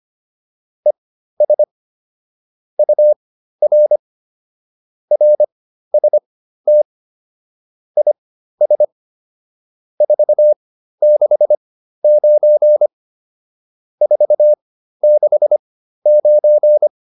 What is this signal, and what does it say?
Morse code, heard as a keyed tone, about 25 wpm, spaced as for 13 wpm, 600 Hz. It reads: ES UR RST IS 469 469